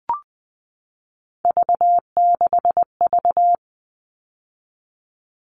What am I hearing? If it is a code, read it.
Morse code, heard as a keyed tone, about 20 wpm, 700 Hz. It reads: V6V
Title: All Cut Numbers - 3-Digit 20wpm